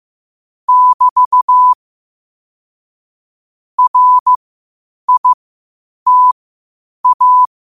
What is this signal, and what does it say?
Morse code, heard as a keyed tone, about 15 wpm, spaced as for 8 wpm, 1000 Hz. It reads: <BT> RITA
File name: qotdfiles / 8wpm